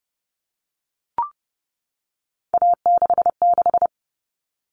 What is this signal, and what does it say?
Morse code, heard as a keyed tone, about 30 wpm, 700 Hz. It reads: A66
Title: All Cut Numbers - 3-Digit 30wpm